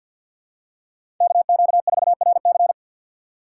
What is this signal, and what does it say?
Morse code, heard as a keyed tone, about 53 wpm, 700 Hz. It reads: KQ4RC
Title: USA Call Signs - 5 Characters - 6x Character Spacing - 50wpm